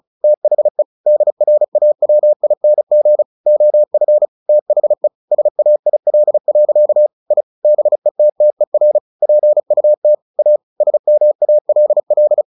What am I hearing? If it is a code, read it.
Morse code, heard as a keyed tone, about 34 wpm, 600 Hz. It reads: THE DRAWING OF THE SAIL. I BETTER PUT A SMALL
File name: the old man and the sea 2 35wpm